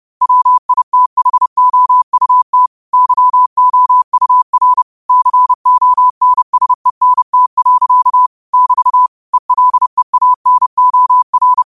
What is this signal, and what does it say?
Morse code, heard as a keyed tone, about 30 wpm, 1000 Hz. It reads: WITHOUT YOUR CONSENT. <BT> ELEANOR